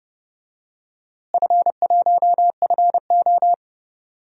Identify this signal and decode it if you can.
Morse code, heard as a keyed tone, about 32 wpm, 700 Hz. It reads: F1FO
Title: DXCC-100 Call Signs - 4 Characters - Repeated 3x - 30wpm